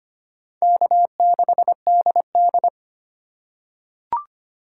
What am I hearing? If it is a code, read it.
Morse code, heard as a keyed tone, about 25 wpm, 700 Hz. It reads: K6DD
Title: USA Call Signs - 4 Characters - 2x Character Spacing - 25wpm